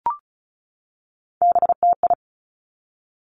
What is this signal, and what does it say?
Morse code, heard as a keyed tone, about 36 wpm, 700 Hz. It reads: BTI
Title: Three Challenging Letter-Number Combinations 35wpm